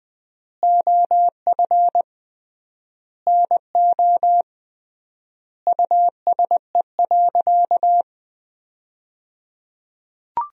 Morse code, heard as a keyed tone, about 20 wpm, 700 Hz. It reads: OF NO USE.
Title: 3x Word-Spacing - Sentences from Top 100 Words - 20wpm